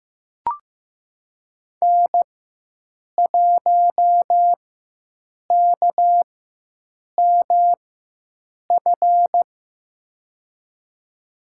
Morse code, heard as a keyed tone, about 15 wpm, spaced as for 7 wpm, 700 Hz. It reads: N1KMF
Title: USA Call Signs - 5 Characters - 4x Character Spacing - 15wpm